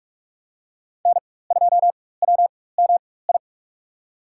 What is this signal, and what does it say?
Morse code, heard as a keyed tone, about 47 wpm, spaced as for 21 wpm, 700 Hz. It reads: N2WMI
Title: USA Call Signs - 4x Character Spacing - 45wpm